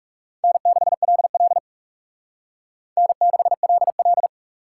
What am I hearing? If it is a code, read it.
Morse code, heard as a keyed tone, about 38 wpm, 700 Hz. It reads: N6LL N6LL